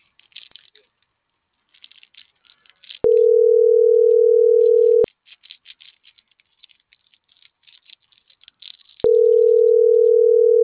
Ringback tone (North America), 440 + 480 Hz.
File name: CENTRAL GENERAL / out tania.lemus - .69